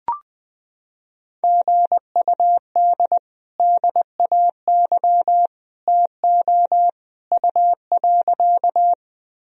Morse code, heard as a keyed tone, about 20 wpm, 700 Hz. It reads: GUD DAY TO U.
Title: Sentences from Top 200 Words and Abbreviations 20wpm